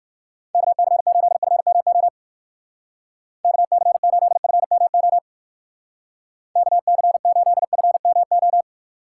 Morse code, heard as a keyed tone, about 59 wpm, 700 Hz. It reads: KC8FMO KC8FMO KC8FMO